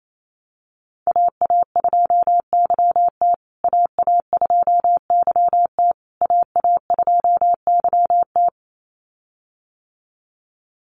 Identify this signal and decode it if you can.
Morse code, heard as a keyed tone, about 28 wpm, 700 Hz. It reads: AA2YT AA2YT AA2YT
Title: USA Call Signs Repeated 3x 28wpm